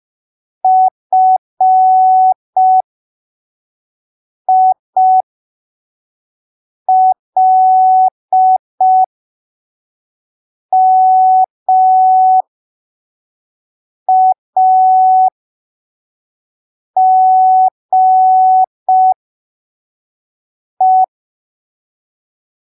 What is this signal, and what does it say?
Morse code, heard as a keyed tone, about 5 wpm, spaced as for 3 wpm, 750 Hz. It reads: FILMAGE
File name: arquivos / texto / texto 09 05ppm